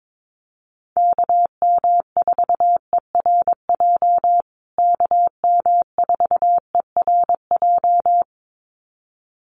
Morse code, heard as a keyed tone, about 22 wpm, 700 Hz. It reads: KM4ERJ KM4ERJ
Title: USA Call Signs Repeated 2x 22wpm